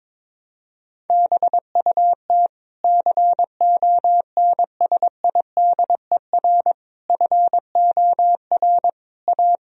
Morse code, heard as a keyed tone, about 22 wpm, 700 Hz. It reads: BUT CONSIDER FOR A
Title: Sentences from Top 1000 Words - Review Part 4 - 22wpm